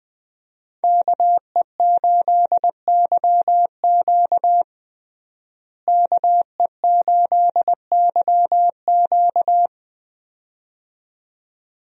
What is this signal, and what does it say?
Morse code, heard as a keyed tone, about 20 wpm, 700 Hz. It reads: KE8YQ KE8YQ
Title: USA Call Signs - 5 Characters - Repeated 2x - 20wpm